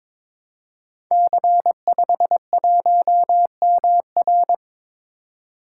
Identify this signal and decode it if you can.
Morse code, heard as a keyed tone, about 22 wpm, 700 Hz. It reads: C51MR
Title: Code Groups - 5 Letter:Numbers - 22wpm